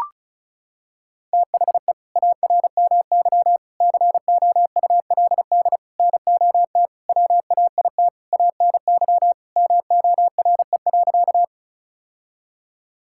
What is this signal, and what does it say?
Morse code, heard as a keyed tone, about 34 wpm, 700 Hz. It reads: THE ARMY COULD NOT WAIT ANY MORE.
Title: Sentences from Top 1000 Words - Review Part 5 - 35wpm